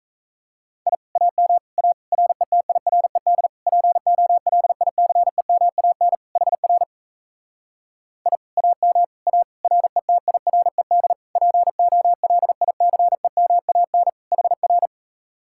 Morse code, heard as a keyed tone, about 38 wpm, 700 Hz. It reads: I AM A RETIRED POLICEMAN HR I AM A RETIRED POLICEMAN HR